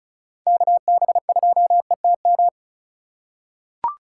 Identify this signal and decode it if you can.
Morse code, heard as a keyed tone, about 36 wpm, 700 Hz. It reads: KB2ETM
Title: USA Call Signs - 3x Character Spacing - 35wpm